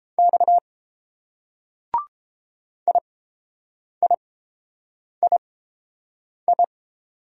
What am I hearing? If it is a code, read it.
Morse code, heard as a keyed tone, about 30 wpm, spaced as for 7 wpm, 700 Hz. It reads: X IIII